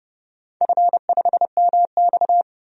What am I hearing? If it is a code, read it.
Morse code, heard as a keyed tone, about 30 wpm, 700 Hz. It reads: F5MX
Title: DXCC-100 Call Signs - 4 Characters - 30wpm